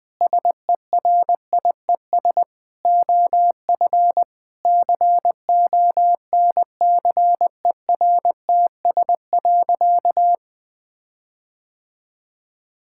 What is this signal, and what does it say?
Morse code, heard as a keyed tone, about 20 wpm, 700 Hz. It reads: SERIES OF CONCERTS.
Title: Sentences with 8-Word Phrases - 20wpm